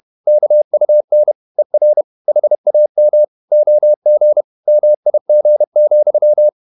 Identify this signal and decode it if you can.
Morse code, heard as a keyed tone, about 31 wpm, 600 Hz. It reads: KUN ER HAM OG MIG,
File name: Ferie QRQ 31wpm